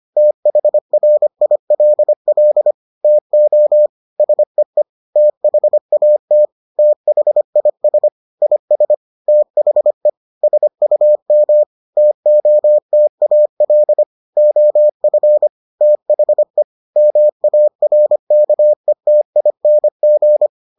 Morse code, heard as a keyed tone, about 25 wpm, 600 Hz. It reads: THRILL TO SEE THAT THIS IS THE SUM TOTAL OF THE MARKETING